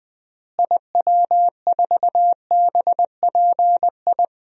Morse code, heard as a keyed tone, about 20 wpm, 700 Hz. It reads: IW4BPI
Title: DXCC-100 Call Signs - 6 Characters - Repeated 2x - 20wpm